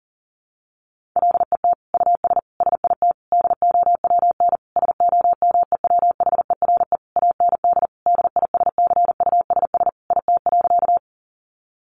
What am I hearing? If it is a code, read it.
Morse code, heard as a keyed tone, about 40 wpm, 700 Hz. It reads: LET US SIT DOWN SOMEWHERE AND DISCUSS IT.
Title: Sentences from Top 1400 Words 40wpm